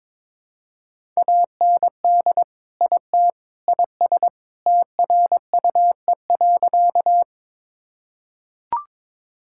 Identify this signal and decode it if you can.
Morse code, heard as a keyed tone, about 23 wpm, 700 Hz. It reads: AND IT IS TRUE.
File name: practice / 22wpm / Sentences with 3-Word Phrases - 22wpm